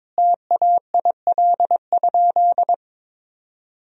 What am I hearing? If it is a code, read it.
Morse code, heard as a keyed tone, about 21 wpm, 700 Hz. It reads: TAIL?
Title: Sentences from Top 1000 Words - Review Part 3 - 22wpm